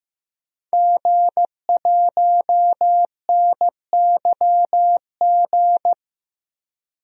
Morse code, heard as a keyed tone, about 15 wpm, 700 Hz. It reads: G1NYG
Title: DXCC-100 Call Signs - 5 Characters - 15wpm